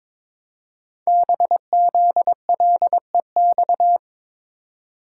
Morse code, heard as a keyed tone, about 23 wpm, 700 Hz. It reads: BZLEX
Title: Code Groups - 5 Letters - 22wpm